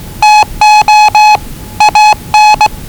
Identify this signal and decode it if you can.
Morse code, heard as a keyed tone, about 20 wpm, 840 Hz. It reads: TO AN
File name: WellingtonRadioZlwCloseDownOnHfAndMf / 01-Zlw 4277khz 417.5 qrt until 0925z